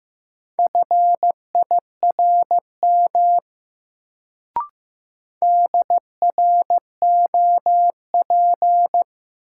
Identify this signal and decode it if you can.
Morse code, heard as a keyed tone, about 15 wpm, 700 Hz. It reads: FIRM DROP
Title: Sentences from Top 1400 Words - New Word Drill Alt - 15wpm